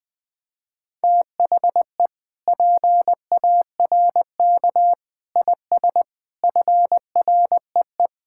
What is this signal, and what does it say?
Morse code, heard as a keyed tone, about 20 wpm, 700 Hz. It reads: THE PARK IS FREE
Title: Sentences with 7-Word Phrases - 20wpm